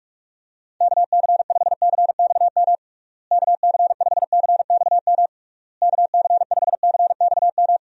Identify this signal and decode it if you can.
Morse code, heard as a keyed tone, about 46 wpm, 700 Hz. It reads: KC5CXM KC5CXM KC5CXM
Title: USA Call Signs - 6 Characters - Repeated 3x - 45wpm